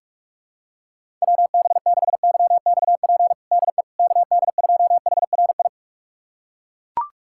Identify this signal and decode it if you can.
Morse code, heard as a keyed tone, about 41 wpm, 700 Hz. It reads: WB6YXP DE KD2HRI
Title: USA Callsign DE Callsign 45wpm